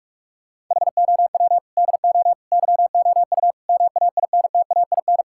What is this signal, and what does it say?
Morse code, heard as a keyed tone, about 41 wpm, 700 Hz. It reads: HOW DO YOU MAINTAIN
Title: Sentences from Top 1000 Words - Review Part 6 - 45wpm